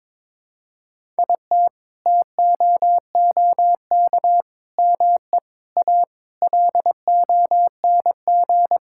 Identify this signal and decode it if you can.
Morse code, heard as a keyed tone, about 22 wpm, 700 Hz. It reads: IT TOOK ME A LONG
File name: practice / 22wpm / Sets of 7 Words - 22wpm